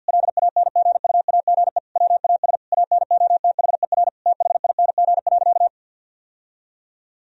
Morse code, heard as a keyed tone, about 50 wpm, 700 Hz. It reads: LANGUAGE WAS ANOTHER THING.
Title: Sentences from Top 1000 Words - Review Part 3 - 50wpm